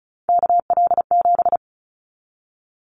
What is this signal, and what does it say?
Morse code, heard as a keyed tone, about 34 wpm, 700 Hz. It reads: KL7
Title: Callsign Prefixes for Top 100 DXCC - 35wpm